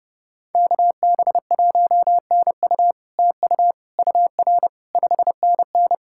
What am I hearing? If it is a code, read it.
Morse code, heard as a keyed tone, about 30 wpm, 700 Hz. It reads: KB1NU TU UR 5NN
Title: SOTA QSO Practice 30wpm